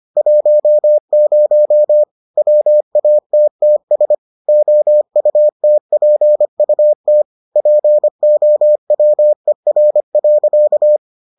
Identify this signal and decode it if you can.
Morse code, heard as a keyed tone, about 25 wpm, 600 Hz. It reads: 10 WATTS OUTPUT POWER.